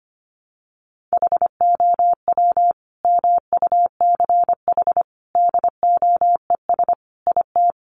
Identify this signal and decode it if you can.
Morse code, heard as a keyed tone, about 25 wpm, 700 Hz. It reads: HOW MUCH DOES IT
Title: Sentences from Top 700 Words 25wpm